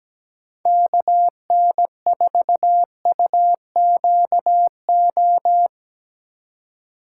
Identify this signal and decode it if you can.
Morse code, heard as a keyed tone, about 20 wpm, 700 Hz. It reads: KN4UQO
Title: Speed Racing - USA 2x3 Call Signs 17wpm